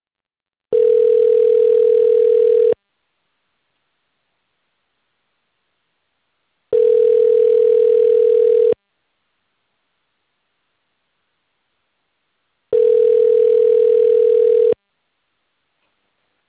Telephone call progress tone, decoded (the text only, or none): ringback tone